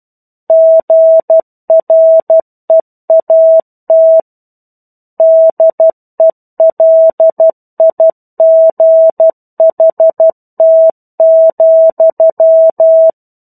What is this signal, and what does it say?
Morse code, heard as a keyed tone, about 12 wpm, 650 Hz. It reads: GREAT DELIGHT,
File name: Chapter-01 12 WPM